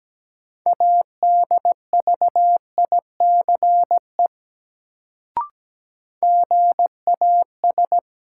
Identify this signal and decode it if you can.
Morse code, heard as a keyed tone, about 17 wpm, 700 Hz. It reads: ADVICE GAS
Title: Sentences from Top 1100 Words - New Word Drill - 17wpm